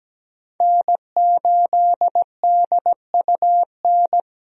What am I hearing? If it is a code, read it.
Morse code, heard as a keyed tone, about 17 wpm, 700 Hz. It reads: N8DUN
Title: USA Call Signs - 5 Characters - 17wpm